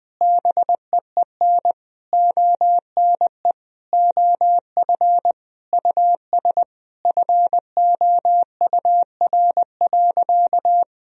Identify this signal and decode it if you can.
Morse code, heard as a keyed tone, about 20 wpm, 700 Hz. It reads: BEEN ONE OF US FOUR.
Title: Sentences from Top 1000 Words - Review Part 1 - 20wpm